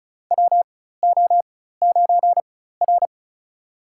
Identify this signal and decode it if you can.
Morse code, heard as a keyed tone, about 34 wpm, spaced as for 16 wpm, 700 Hz. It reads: WO9R
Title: USA Call Signs - 4 Characters - 4x Character Spacing - 35wpm